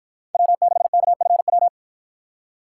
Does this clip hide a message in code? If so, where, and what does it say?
Yes: Morse code from 0.3 to 1.7 s, heard as a keyed tone, about 53 wpm, 700 Hz. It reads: W6KFW